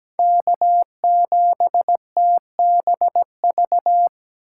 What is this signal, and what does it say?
Morse code, heard as a keyed tone, about 17 wpm, 700 Hz. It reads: K7TBV
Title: USA Call Signs - 5 Characters - 2x Character Spacing - 17wpm